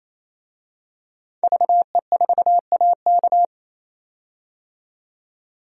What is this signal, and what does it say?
Morse code, heard as a keyed tone, about 27 wpm, 700 Hz. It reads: VE4AK
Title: DXCC-100 Call Signs - 5 Characters - 28wpm